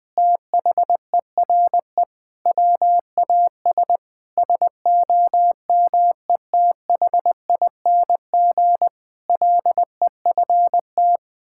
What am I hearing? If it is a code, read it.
Morse code, heard as a keyed tone, about 21 wpm, 700 Hz. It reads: THERE WAS SOMETHING LEFT
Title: Sentences from Top 1000 Words - Review Part 6 - 20wpm